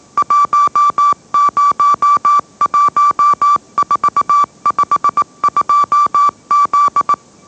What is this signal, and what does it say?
Morse code, heard as a keyed tone, about 18 wpm, 1160 Hz. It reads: 101452Z
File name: mrx / mrx082 20wpm